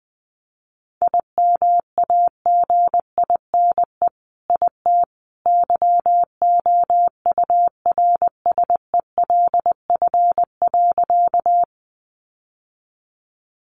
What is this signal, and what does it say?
Morse code, heard as a keyed tone, about 20 wpm, 700 Hz. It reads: IMAGINE IT YOURSELF.
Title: Sentences from Top 1400 Words 20wpm